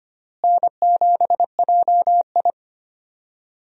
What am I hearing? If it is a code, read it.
Morse code, heard as a keyed tone, about 25 wpm, 700 Hz. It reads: N7JI
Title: USA Call Signs - 4 Characters - Repeated 3x - 25wpm